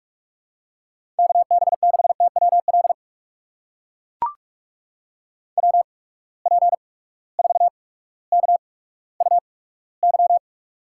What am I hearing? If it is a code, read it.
Morse code, heard as a keyed tone, about 43 wpm, spaced as for 19 wpm, 700 Hz. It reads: KB6TWL WP4KUY
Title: USA Call Signs - 6 Characters - 8x Character Spacing - 45wpm